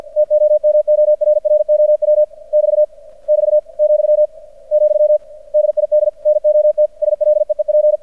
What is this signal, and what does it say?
Morse code, heard as a keyed tone, about 50 wpm, 600 Hz. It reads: TOMORROW <BT> <BT> , , DID NOT SLEEP
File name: archived / CRSnet